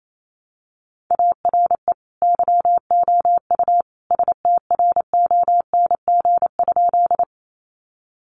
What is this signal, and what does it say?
Morse code, heard as a keyed tone, about 28 wpm, 700 Hz. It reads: ARE YOU STRONG?
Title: Sentences from Top 500 Words 28wpm